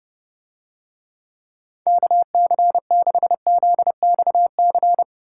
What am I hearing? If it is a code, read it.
Morse code, heard as a keyed tone, about 28 wpm, 700 Hz. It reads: KC6ZXC